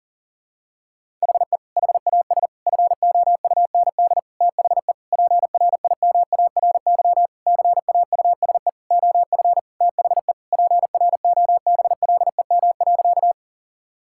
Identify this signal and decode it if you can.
Morse code, heard as a keyed tone, about 40 wpm, 700 Hz. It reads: HE HAS FOUND THE PRIMARY CAUSE OF THE PROBLEM.
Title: Sentences from Top 1000 Words - Review Part 5 - 40wpm